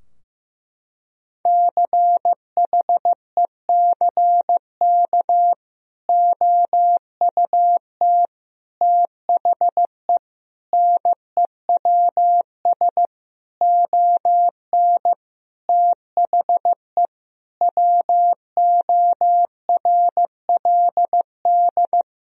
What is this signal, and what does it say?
Morse code, heard as a keyed tone, about 15 wpm, 700 Hz. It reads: CHECK OUT THE NEWS ON THE WORLD
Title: Sentences from Top 1000 Words - Review Part 4 - 15wpm